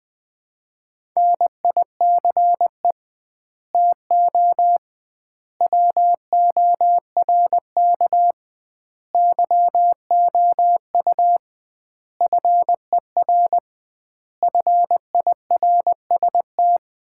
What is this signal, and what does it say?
Morse code, heard as a keyed tone, about 20 wpm, 700 Hz. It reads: NICE TO WORK YOU FER FIRST
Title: Common QSO Phrases - 2x Word-Spacing - 20wpm